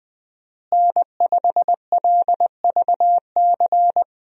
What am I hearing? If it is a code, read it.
Morse code, heard as a keyed tone, about 20 wpm, 700 Hz. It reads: N5LVC
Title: USA Call Signs - 5 Characters - Repeated 2x - 20wpm